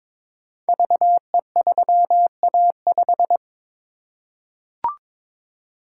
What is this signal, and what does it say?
Morse code, heard as a keyed tone, about 22 wpm, 700 Hz. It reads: VE3A5
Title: Code Groups - 5 Letter:Numbers - 22wpm